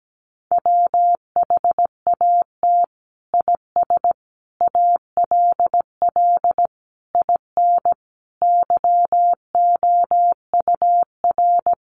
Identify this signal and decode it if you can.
Morse code, heard as a keyed tone, about 17 wpm, 700 Hz. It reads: WHAT IS ALL IN YOUR
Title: Sentences from Top 1200 Words 17wpm